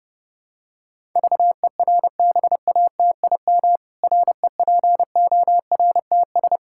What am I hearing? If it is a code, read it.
Morse code, heard as a keyed tone, about 32 wpm, 700 Hz. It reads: VERBATIM REPORTS
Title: Pangram Challenge v2 - 30wpm